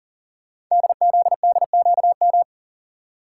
Morse code, heard as a keyed tone, about 39 wpm, 700 Hz. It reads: DZDQM
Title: Code Groups - 5 Letters - 40wpm